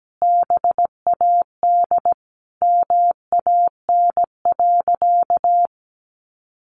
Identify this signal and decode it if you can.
Morse code, heard as a keyed tone, about 17 wpm, 700 Hz. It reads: BAD MAN.
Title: Sentences from Top 1200 Words 17wpm